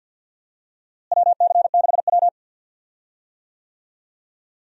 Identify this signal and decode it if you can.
Morse code, heard as a keyed tone, about 53 wpm, 700 Hz. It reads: WC6W
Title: USA Call Signs - 4 Characters - Repeated 2x - 50wpm